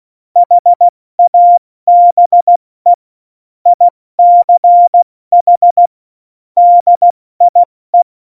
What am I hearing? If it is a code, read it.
Morse code, heard as a keyed tone, about 16 wpm, 700 Hz. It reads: HABE ICH DIE
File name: Morsekurs / Witze / Witz 05. Januar 2021, 70 F80